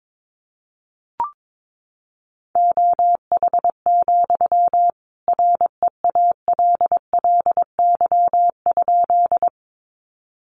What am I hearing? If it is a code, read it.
Morse code, heard as a keyed tone, about 22 wpm, 700 Hz. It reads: OH, REALLY?
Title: Sentences from Top 600 Words 22wpm